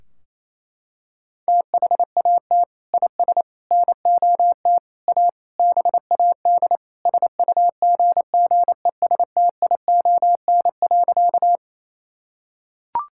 Morse code, heard as a keyed tone, about 28 wpm, 700 Hz. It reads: THAT IS NOT A BAD SUGGESTION.
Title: Sentences from Top 1600 Words 28wpm